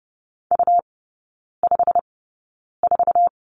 Morse code, heard as a keyed tone, about 30 wpm, spaced as for 9 wpm, 700 Hz. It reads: U54
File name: Lesson 17e - Letter u - Three Characters 30wpm